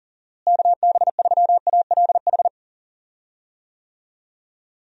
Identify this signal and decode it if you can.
Morse code, heard as a keyed tone, about 40 wpm, 700 Hz. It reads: KB3ALH
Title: USA Call Signs - 6 Characters - Repeated 3x - 40wpm